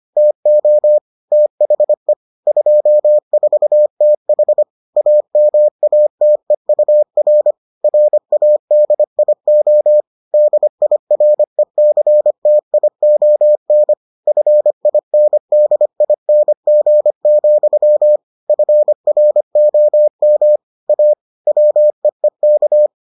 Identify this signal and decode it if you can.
Morse code, heard as a keyed tone, about 25 wpm, 600 Hz. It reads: TO THE 24TH AMATEUR RADIO DIRECTION FINDING, FROM A WEEK